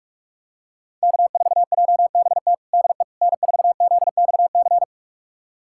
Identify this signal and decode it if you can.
Morse code, heard as a keyed tone, about 46 wpm, 700 Hz. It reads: K4JBT DE N4ZXC
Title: USA Callsign DE Callsign - 5 Characters - 45wpm